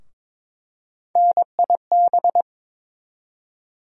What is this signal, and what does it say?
Morse code, heard as a keyed tone, about 22 wpm, 700 Hz. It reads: NIB
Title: USA Call Sign Postfixes - 3 Letters 22wpm